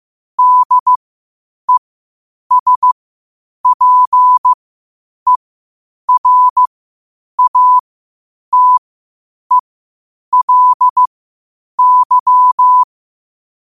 Morse code, heard as a keyed tone, about 15 wpm, spaced as for 8 wpm, 1000 Hz. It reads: DESPERATELY
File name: 8wpm